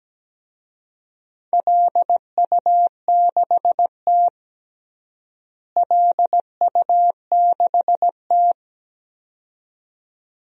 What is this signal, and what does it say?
Morse code, heard as a keyed tone, about 17 wpm, 700 Hz. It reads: LU6T LU6T
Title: DXCC-100 Call Signs - 4 Characters - Repeated 2x - 17wpm